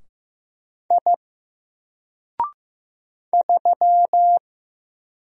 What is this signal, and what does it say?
Morse code, heard as a keyed tone, about 15 wpm, 700 Hz. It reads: I 3
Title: Single Letter-Number 15wpm